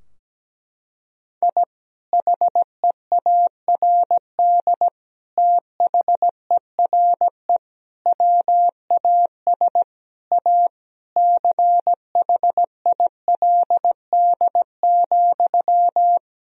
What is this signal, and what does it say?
Morse code, heard as a keyed tone, about 17 wpm, 700 Hz. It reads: I HEARD THERE WAS A CHILD,
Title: Sentences from Top 1000 Words - Review Part 2 - 17wpm